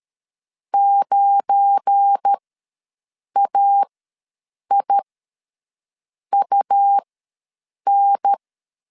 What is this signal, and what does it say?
Morse code, heard as a keyed tone, about 13 wpm, spaced as for 6 wpm, 800 Hz. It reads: 9AIUN
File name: LEZIONI / 7 GF9Z / 7 GF9Z ws 12wpm 3321